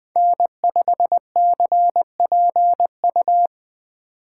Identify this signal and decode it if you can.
Morse code, heard as a keyed tone, about 20 wpm, 700 Hz. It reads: N5CPU
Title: USA Call Signs - 8x Character Spacing - 20wpm